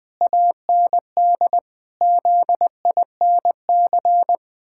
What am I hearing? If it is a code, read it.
Morse code, heard as a keyed tone, about 20 wpm, 700 Hz. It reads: AND ZINC
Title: Pangram Challenge v2 - 20wpm